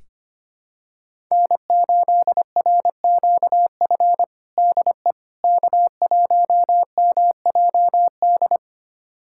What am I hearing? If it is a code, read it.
Morse code, heard as a keyed tone, about 25 wpm, 700 Hz. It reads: N8RQF DE K1MJD
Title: USA Callsign DE Callsign - 5 Characters - 25wpm